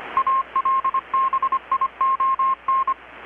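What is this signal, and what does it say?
Morse code, heard as a keyed tone, about 25 wpm, 1070 Hz. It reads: ALBION
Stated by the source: Morse from KPH's Night of Nights XX maritime radiotelegraphy event, received in Chiapas, Mexico about 3,800 km away on a Kenwood R-600 with a grounded 10 m long-wire antenna.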